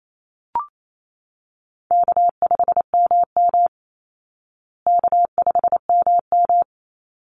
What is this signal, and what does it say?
Morse code, heard as a keyed tone, about 28 wpm, 700 Hz. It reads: K5MM K5MM
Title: Easy Call Signs Repeated 3x - 28wpm